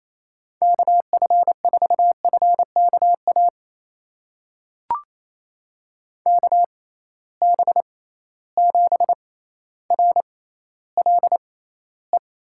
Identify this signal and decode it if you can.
Morse code, heard as a keyed tone, about 28 wpm, spaced as for 14 wpm, 700 Hz. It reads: KF4FKA KB7RLE
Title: USA Call Signs - 6 Characters - 6x Character Spacing - 28wpm